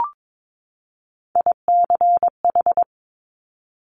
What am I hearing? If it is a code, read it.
Morse code, heard as a keyed tone, about 22 wpm, 700 Hz. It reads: ICH
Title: Top Three Letter Combinations in Words 22wpm